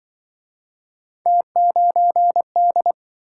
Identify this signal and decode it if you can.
Morse code, heard as a keyed tone, about 27 wpm, 700 Hz. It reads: T9D